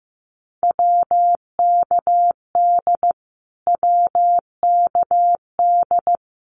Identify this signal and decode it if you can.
Morse code, heard as a keyed tone, about 15 wpm, 700 Hz. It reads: WKD WKD
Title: Abbreviations Repeated 3x - 15wpm